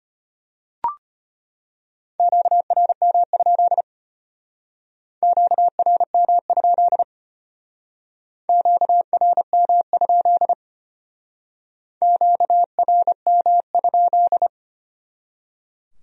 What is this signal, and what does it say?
Morse code, heard as a keyed tone, about 32 wpm, 700 Hz. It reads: QRM? QRM? QRM? QRM?